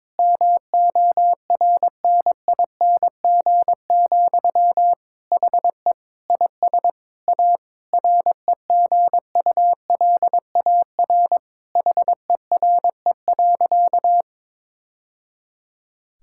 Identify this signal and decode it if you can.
Morse code, heard as a keyed tone, about 22 wpm, 700 Hz. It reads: MORNING, HE IS A REGULAR HERE.
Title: Sentences from Top 1000 Words - Review Part 6 - 22wpm